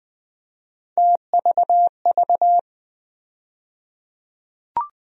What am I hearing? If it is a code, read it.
Morse code, heard as a keyed tone, about 20 wpm, 700 Hz. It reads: TVV
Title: USA Call Sign Postfixes - 3 Letters 20wpm